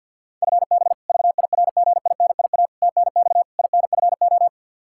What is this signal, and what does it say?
Morse code, heard as a keyed tone, about 50 wpm, 700 Hz. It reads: FB VIRGINIA TNX INFO
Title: K1USN SST QSO Practice 50wpm